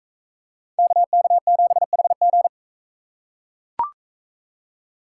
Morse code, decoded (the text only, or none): KK7HG